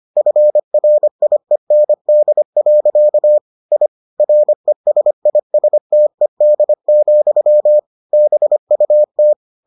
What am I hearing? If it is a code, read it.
Morse code, heard as a keyed tone, about 25 wpm, 600 Hz. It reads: FRIEND. I RESISTED, BUT